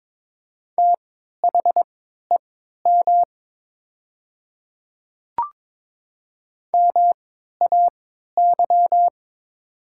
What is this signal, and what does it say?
Morse code, heard as a keyed tone, about 22 wpm, spaced as for 12 wpm, 700 Hz. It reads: THEM MAY
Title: 3x Character Spacing - Top 100 Words - 22wpm